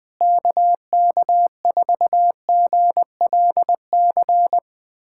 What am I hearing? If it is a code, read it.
Morse code, heard as a keyed tone, about 20 wpm, 700 Hz. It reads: KK4GLC
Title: USA Callsign DE Callsign 20wpm